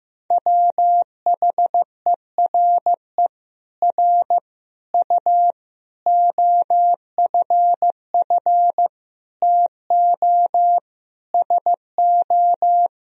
Morse code, heard as a keyed tone, about 15 wpm, 700 Hz. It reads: WHERE R U OFF TO SO
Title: Sentences from Top 400 Words and Abbreviations 15wpm